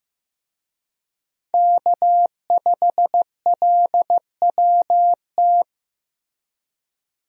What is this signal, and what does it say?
Morse code, heard as a keyed tone, about 15 wpm, 700 Hz. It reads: K5LWT
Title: USA Call Signs - 5 Characters - Repeated 3x - 15wpm